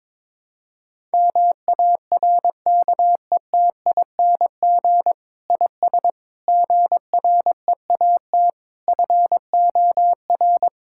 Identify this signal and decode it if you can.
Morse code, heard as a keyed tone, about 22 wpm, 700 Hz. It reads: MARKETING IS GREAT FOR
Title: Sentences from Top 1000 Words - Review Part 5 - 22wpm